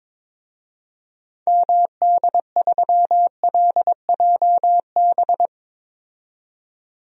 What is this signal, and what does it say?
Morse code, heard as a keyed tone, about 21 wpm, 700 Hz. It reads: MD3LJB
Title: DXCC-100 Call Signs - 6 Characters - 22wpm